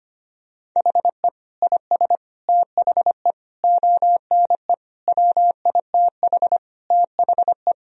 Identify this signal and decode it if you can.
Morse code, heard as a keyed tone, about 25 wpm, 700 Hz. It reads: HE IS THE ONE WITH THE
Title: Sentences from Top 1000 Words - Review Part 3 - 25wpm